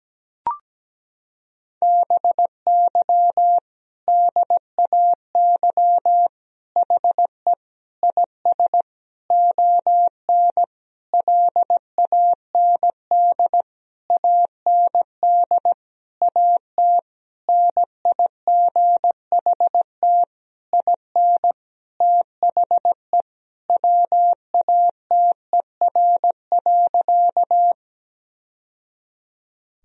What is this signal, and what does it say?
Morse code, heard as a keyed tone, about 17 wpm, 700 Hz. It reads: BY DAY HE IS ON LAND AND AT NIGHT IN THE WATER.
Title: Sentences from Top 1000 Words - Review Part 1 - 17wpm